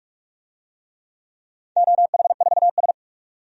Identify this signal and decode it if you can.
Morse code, heard as a keyed tone, about 47 wpm, 700 Hz. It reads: OH4S
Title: DXCC-100 Call Signs - 4 Characters - 45wpm